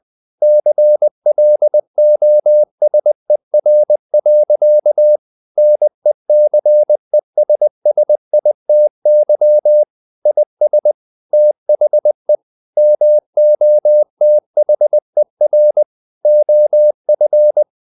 Morse code, heard as a keyed tone, about 20 wpm, 600 Hz. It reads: CLOSER. NECESSITY IS THE MOTHER OF
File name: CW / Practice / EnglishSayings20